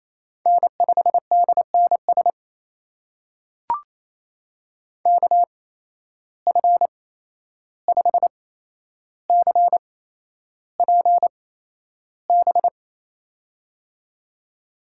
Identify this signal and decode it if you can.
Morse code, heard as a keyed tone, about 28 wpm, spaced as for 11 wpm, 700 Hz. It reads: N5DNS KF5CPB
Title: USA Call Signs - 8x Character Spacing - 28wpm